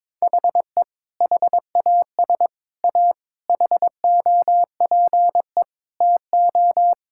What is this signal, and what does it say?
Morse code, heard as a keyed tone, about 22 wpm, 700 Hz. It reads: HE HAS A HOPE TO
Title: Sentences from Top 1000 Words - Review Part 6 - 22wpm